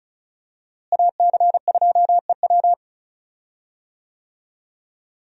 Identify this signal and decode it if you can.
Morse code, heard as a keyed tone, about 34 wpm, 700 Hz. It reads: AC2EW
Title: USA Call Signs - 5 Characters - Repeated 3x - 35wpm